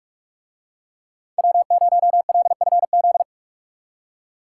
Morse code, heard as a keyed tone, about 45 wpm, 700 Hz. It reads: W0LFZ